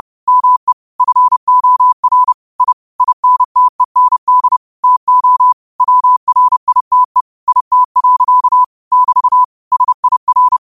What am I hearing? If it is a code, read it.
Morse code, heard as a keyed tone, about 30 wpm, 1000 Hz. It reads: ME FOR I INTEND TO WRITE IT. <BT> SIR
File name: qotdfiles / 30wpm